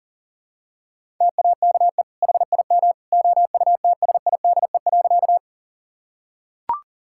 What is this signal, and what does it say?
Morse code, heard as a keyed tone, about 40 wpm, 700 Hz. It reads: TAKE HIM OUTSIDE.
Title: Sentences from Top 1000 Words - Review Part 6 - 40wpm